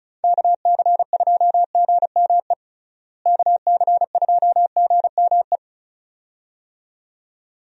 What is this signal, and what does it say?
Morse code, heard as a keyed tone, about 36 wpm, 700 Hz. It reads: KC2GME KC2GME
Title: USA Call Signs - 6 Characters - Repeated 2x - 35wpm